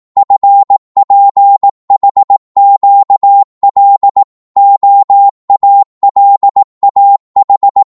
Morse code, heard as a keyed tone, about 18 wpm, 800 Hz. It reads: FPHQL OALAH
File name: Morse / ltr-18wpm